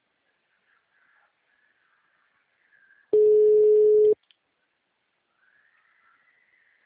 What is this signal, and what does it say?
Ringback tone (Europe, 425 Hz).
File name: out- -2012- - - .2633